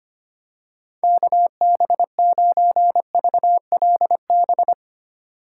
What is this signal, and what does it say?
Morse code, heard as a keyed tone, about 25 wpm, 700 Hz. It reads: KB9VLB
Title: USA Call Signs - 3x Character Spacing - 25wpm